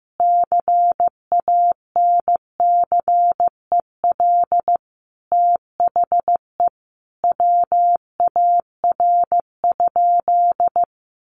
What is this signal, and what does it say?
Morse code, heard as a keyed tone, about 15 wpm, 700 Hz. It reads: CANCEL THE WAR?
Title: Sentences from Top 1200 Words 15wpm